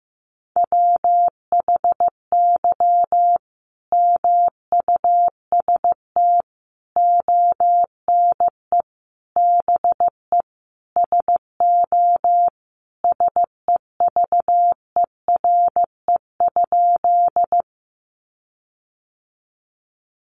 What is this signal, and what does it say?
Morse code, heard as a keyed tone, about 15 wpm, 700 Hz. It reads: WHY MUST ONE BE SO SEVERE?
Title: Sentences from Top 1400 Words 15wpm